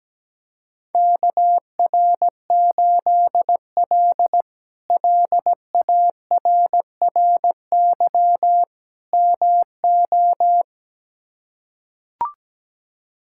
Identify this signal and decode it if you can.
Morse code, heard as a keyed tone, about 17 wpm, 700 Hz. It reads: KR8L LARRY MO
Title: K1USN SST QSO Practice - 17wpm